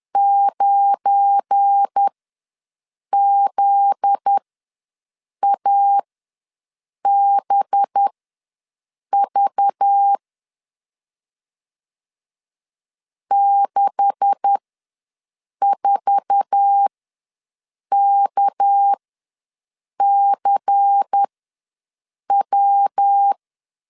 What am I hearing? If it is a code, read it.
Morse code, heard as a keyed tone, about 10 wpm, spaced as for 6 wpm, 800 Hz. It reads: 9ZABV 64KCW
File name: LEZIONI / 10 SIGNS / 10B INT ws 10wpm 4134